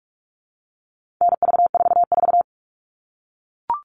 Morse code, heard as a keyed tone, about 44 wpm, 700 Hz. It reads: NV44
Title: All Cut Numbers - 4-Digit 45wpm